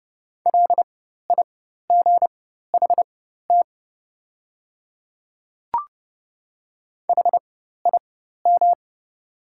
Morse code, heard as a keyed tone, about 30 wpm, spaced as for 14 wpm, 700 Hz. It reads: LIGHT HIM
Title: 4x Character Spacing - Top 200 Words - 30wpm